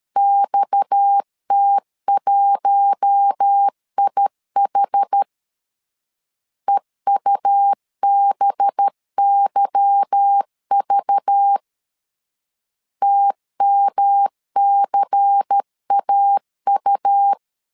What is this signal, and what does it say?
Morse code, heard as a keyed tone, about 13 wpm, 800 Hz. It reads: XT1IH EUBYV TMCAU
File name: LEZIONI / 8 YJ45 / 8 Y 12wpm 2228